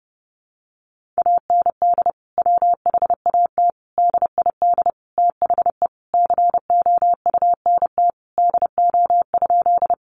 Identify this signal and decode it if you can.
Morse code, heard as a keyed tone, about 30 wpm, 700 Hz. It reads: AND WHAT DID THE COUNT DO?
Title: Sentences from Top 1500 Words 30wpm